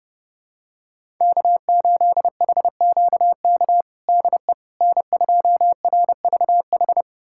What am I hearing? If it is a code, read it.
Morse code, heard as a keyed tone, about 30 wpm, 700 Hz. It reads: K8HQK DE N2RVH
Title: USA Callsign DE Callsign - 5 Characters - 30wpm